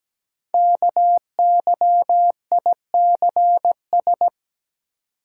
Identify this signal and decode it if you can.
Morse code, heard as a keyed tone, about 17 wpm, 700 Hz. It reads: KYICS
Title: Code Groups - 5 Letters - 17wpm